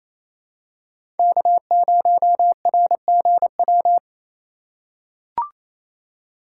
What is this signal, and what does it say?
Morse code, heard as a keyed tone, about 28 wpm, 700 Hz. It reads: K0RGW
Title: USA Call Signs - 5 Characters - 2x Character Spacing - 28wpm